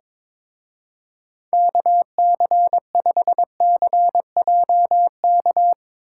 Morse code, heard as a keyed tone, about 24 wpm, 700 Hz. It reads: KC5CJK
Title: Speed Racing - USA 2x3 Call Signs 22wpm